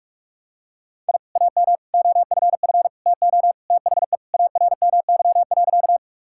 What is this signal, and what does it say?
Morse code, heard as a keyed tone, about 46 wpm, 700 Hz. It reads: I AM OFF TO THE ARMY.
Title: Sentences from Top 1000 Words - Review Part 5 - 45wpm